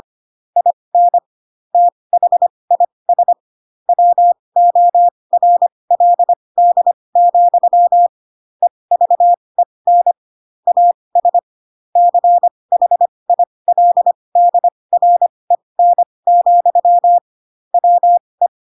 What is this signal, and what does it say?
Morse code, heard as a keyed tone, about 25 wpm, spaced as for 20 wpm, 700 Hz. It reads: IN THIS WORLD, EVEN AS CHILDREN, WE
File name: Pinocchio / Pinocchio ch7 15 25 ik1hge